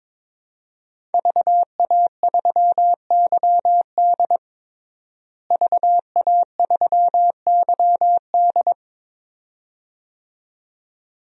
Morse code, heard as a keyed tone, about 22 wpm, 700 Hz. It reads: VA3YD VA3YD
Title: DXCC-100 Call Signs - 5 Characters - Repeated 2x - 22wpm